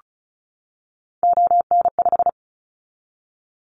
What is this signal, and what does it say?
Morse code, heard as a keyed tone, about 36 wpm, 700 Hz. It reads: ON5
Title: Callsign Prefixes for Top 50 DXCC - 35wpm